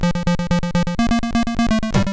There is no code.